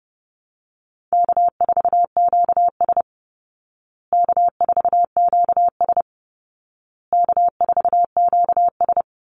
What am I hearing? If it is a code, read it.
Morse code, heard as a keyed tone, about 30 wpm, 700 Hz. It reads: K4QS K4QS K4QS
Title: Call Signs Repeated 2x - 30wpm